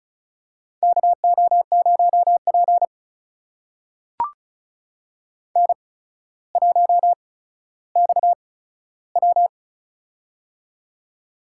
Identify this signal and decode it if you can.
Morse code, heard as a keyed tone, about 36 wpm, spaced as for 15 wpm, 700 Hz. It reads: KO0P N1XW
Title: USA Call Signs - 4 Characters - 8x Character Spacing - 35wpm